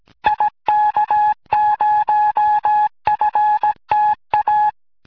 Morse code, heard as a keyed tone, about 17 wpm, 840 Hz. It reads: IK0FTA